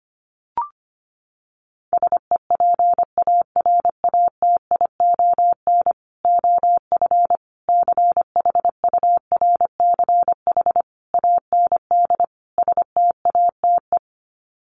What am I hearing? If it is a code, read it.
Morse code, heard as a keyed tone, about 25 wpm, 700 Hz. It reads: SEPARATION OF CHURCH AND STATE
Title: Sets of 5 Words 25wpm